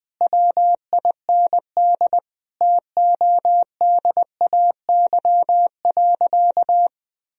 Morse code, heard as a keyed tone, about 21 wpm, 700 Hz. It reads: WIND TODAY.
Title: Sentences from Top 1000 Words - Review Part 2 - 20wpm